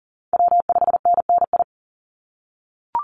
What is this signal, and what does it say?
Morse code, heard as a keyed tone, about 40 wpm, 700 Hz. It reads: W5NNI
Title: USA Call Signs 40wpm